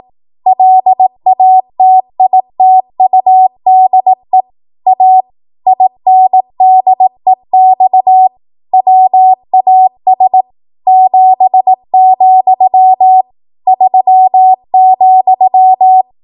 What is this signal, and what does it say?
Morse code, heard as a keyed tone, about 18 wpm, 750 Hz. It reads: LATITUDE A INDEX WAS 7, 3,
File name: Archive / 18 WPM / 18WPM